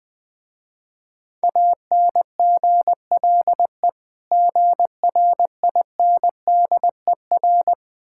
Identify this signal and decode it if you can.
Morse code, heard as a keyed tone, about 21 wpm, 700 Hz. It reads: ANGLE GRINDER
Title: Difficult Words with Hints - 20wpm